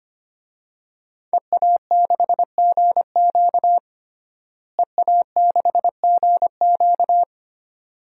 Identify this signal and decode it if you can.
Morse code, heard as a keyed tone, about 25 wpm, 700 Hz. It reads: EA6GQ EA6GQ
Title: DXCC-100 Call Signs - 5 Characters - Repeated 2x - 25wpm